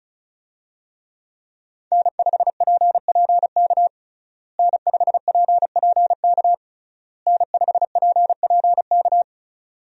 Morse code, heard as a keyed tone, about 36 wpm, 700 Hz. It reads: N5PPK N5PPK N5PPK
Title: USA Call Signs - 5 Characters - Repeated 3x - 35wpm